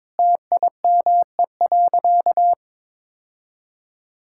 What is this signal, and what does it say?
Morse code, heard as a keyed tone, about 22 wpm, 700 Hz. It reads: TIME.
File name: practice / 22wpm / Sentences with 3-Word Phrases - 22wpm